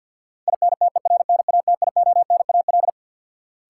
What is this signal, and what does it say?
Morse code, heard as a keyed tone, about 50 wpm, 700 Hz. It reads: INTERNATIONAL
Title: Sentences from Top 1000 Words - Review Part 4 - New Word Drill - 50wpm